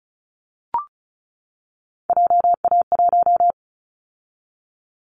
Morse code, heard as a keyed tone, about 36 wpm, 700 Hz. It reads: JA1
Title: Callsign Prefixes for Top 25 DXCC - 35wpm